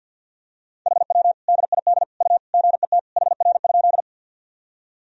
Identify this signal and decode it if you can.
Morse code, heard as a keyed tone, about 46 wpm, 700 Hz. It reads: HW DID U GET HR?
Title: Sentences from Top 200 Words and Abbreviations 50wpm